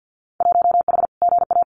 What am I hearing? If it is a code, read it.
Morse code, heard as a keyed tone, about 50 wpm, 700 Hz. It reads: 1H GA
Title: Winter Field Day Exchanges 50wpm